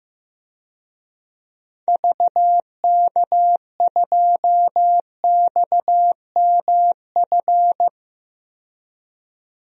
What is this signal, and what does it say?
Morse code, heard as a keyed tone, about 15 wpm, 700 Hz. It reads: VK2XMF
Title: DXCC-100 Call Signs - 6 Characters - 15wpm